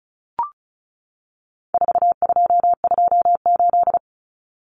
Morse code, heard as a keyed tone, about 34 wpm, 700 Hz. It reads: 4228
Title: Numbers - 4-Digit 35wpm